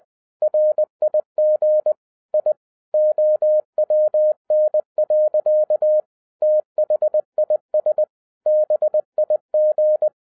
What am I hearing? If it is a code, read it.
Morse code, heard as a keyed tone, about 20 wpm, 610 Hz. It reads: RIG I OWN. THIS BIG
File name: 2011-07-mp3 / n58 BACK FROM A BIG ACCIDENT 20wpm